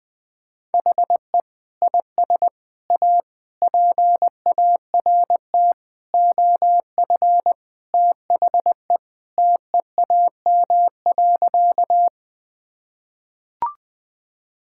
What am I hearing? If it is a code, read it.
Morse code, heard as a keyed tone, about 21 wpm, 700 Hz. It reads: HE IS A PART OF THE TEAM.
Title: Sentences with 4-Word Phrases - 20wpm